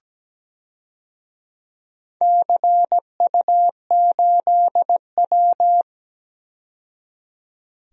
Morse code, heard as a keyed tone, about 17 wpm, 700 Hz. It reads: CU8W
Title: DXCC-100 Call Signs - 4 Characters - 17wpm